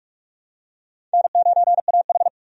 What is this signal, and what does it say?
Morse code, heard as a keyed tone, about 41 wpm, 700 Hz. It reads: N9AH
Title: USA Call Signs - 4 Characters - 45wpm